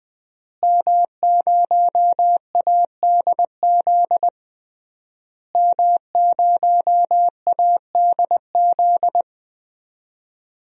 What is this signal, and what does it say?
Morse code, heard as a keyed tone, about 21 wpm, 700 Hz. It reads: M0ADZ M0ADZ
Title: DXCC-100 Call Signs - 5 Characters - Repeated 3x - 20wpm